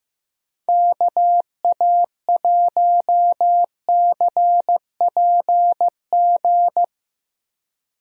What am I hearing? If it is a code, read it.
Morse code, heard as a keyed tone, about 15 wpm, 700 Hz. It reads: KA1CPG
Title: USA Call Signs - 6 Characters - Repeated 2x - 15wpm